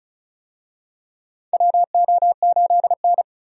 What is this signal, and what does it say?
Morse code, heard as a keyed tone, about 36 wpm, 700 Hz. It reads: WO8N
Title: USA Call Signs - 4 Characters - 35wpm